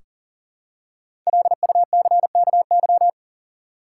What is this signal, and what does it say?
Morse code, heard as a keyed tone, about 37 wpm, 700 Hz. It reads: LUCKY
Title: Five Letter Words - 40wpm